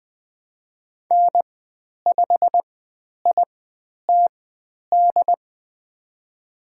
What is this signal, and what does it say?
Morse code, heard as a keyed tone, about 20 wpm, spaced as for 10 wpm, 700 Hz. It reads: N5ITD
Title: 109 Lesson 18g - Letter m - Call Signs 20wpm